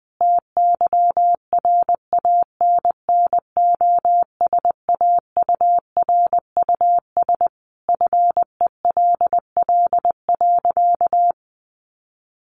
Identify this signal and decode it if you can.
Morse code, heard as a keyed tone, about 20 wpm, 700 Hz. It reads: TYRANNOSAURUS FELL.
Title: A Sound Of Thunder 20wpm